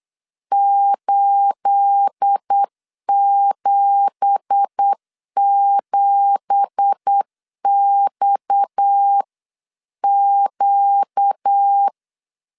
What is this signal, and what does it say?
Morse code, heard as a keyed tone, about 8 wpm, 800 Hz. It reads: 877XQ
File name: CW / DET / 1 9 1500C S B 8wpm 5624